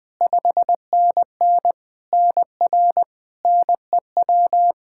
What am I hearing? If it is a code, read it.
Morse code, heard as a keyed tone, about 20 wpm, 700 Hz. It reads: 5NN NR NEW
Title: K1USN SST QSO Practice 20wpm